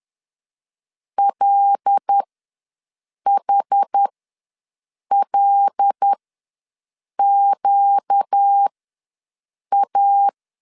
Morse code, heard as a keyed tone, about 10 wpm, spaced as for 6 wpm, 800 Hz. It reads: LHLQA
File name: LEZIONI / 5 LBVC / 5 L ws 10wpm 3636